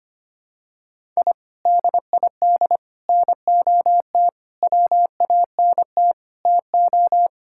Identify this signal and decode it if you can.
Morse code, heard as a keyed tone, about 25 wpm, 700 Hz. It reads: I DID NOT WANT TO
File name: practice / 25wpm / Sentences with 4-Word Phrases - 25wpm